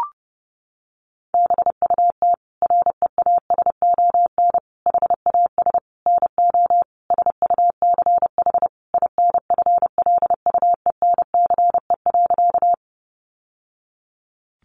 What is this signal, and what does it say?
Morse code, heard as a keyed tone, about 30 wpm, 700 Hz. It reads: BUT REASON HAS NO SUCH INFLUENCE.
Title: Sentences from Top 1100 Words 30wpm